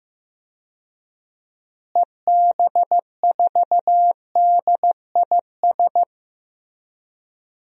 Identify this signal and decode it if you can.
Morse code, heard as a keyed tone, about 15 wpm, 700 Hz. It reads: EB4DIS
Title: DXCC-100 Call Signs - 6 Characters - Repeated 3x - 15wpm